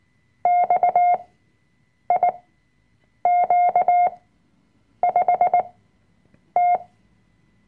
Morse code, heard as a keyed tone, about 19 wpm, spaced as for 7 wpm, 710 Hz. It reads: XIQ5T
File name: Workshop Dateien / CW-Online-Kurs / Lektion 005